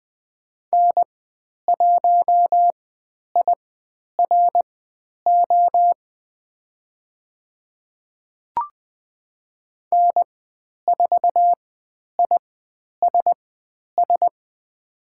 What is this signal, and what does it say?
Morse code, heard as a keyed tone, about 20 wpm, spaced as for 10 wpm, 700 Hz. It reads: N1IRO N4ISS
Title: 109 Lesson 18g - Letter m - Call Signs 20wpm